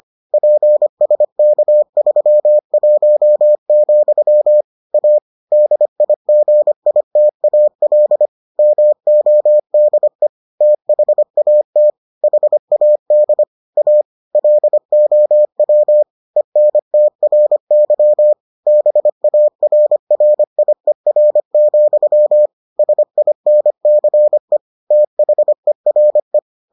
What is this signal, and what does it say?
Morse code, heard as a keyed tone, about 25 wpm, 600 Hz. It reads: PSK31, A DIGITAL MODE THAT HAD A LOW ENTRY BARRIER, SINCE THERE